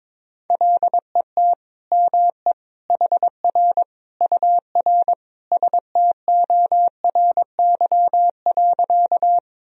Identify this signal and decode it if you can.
Morse code, heard as a keyed tone, about 22 wpm, 700 Hz. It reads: LET ME HR UR STORY.
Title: Sentences from Top 300 Words and Abbreviations 22wpm